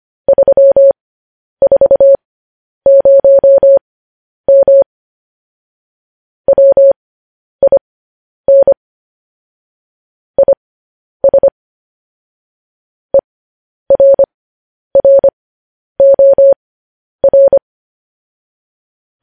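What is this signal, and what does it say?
Morse code, heard as a keyed tone, about 25 wpm, spaced as for 10 wpm, 550 Hz. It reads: 340M WIN IS ERROR
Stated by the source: Morse code news headlines sent at 10 wpm for practice.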